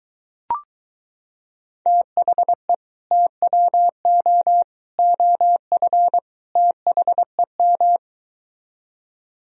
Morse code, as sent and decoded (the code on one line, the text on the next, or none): - .... / - .-- --- / --- ..-. / - .... --
THE TWO OF THEM